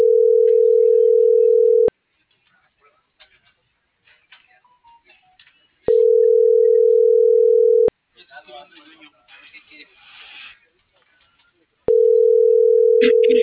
Ringback tone (North America), 440 + 480 Hz.